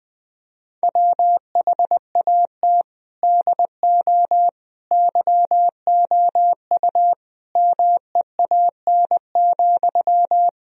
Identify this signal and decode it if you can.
Morse code, heard as a keyed tone, about 21 wpm, 700 Hz. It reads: WHAT DO YOU MEAN,
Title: Sentences from Top 1000 Words - Review Part 4 - 20wpm